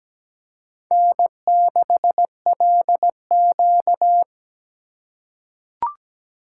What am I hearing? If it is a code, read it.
Morse code, heard as a keyed tone, about 17 wpm, 700 Hz. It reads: N6LQ
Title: USA Call Signs - 4 Characters - 3x Character Spacing - 17wpm